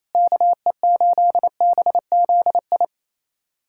Morse code, heard as a keyed tone, about 28 wpm, 700 Hz. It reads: KE8BZI
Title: USA Callsign DE Callsign 28wpm